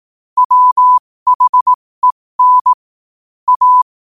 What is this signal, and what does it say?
Morse code, heard as a keyed tone, about 18 wpm, spaced as for 15 wpm, 1000 Hz. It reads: WHEN A
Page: New quotes every day in morse code at 15 Words per minute.